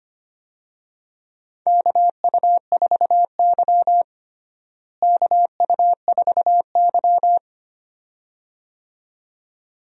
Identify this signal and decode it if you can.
Morse code, heard as a keyed tone, about 25 wpm, 700 Hz. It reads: KU4Y KU4Y
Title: USA Call Signs - 4 Characters - Repeated 2x - 25wpm